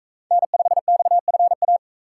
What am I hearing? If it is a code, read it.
Morse code, heard as a keyed tone, about 42 wpm, 700 Hz. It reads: N5XFA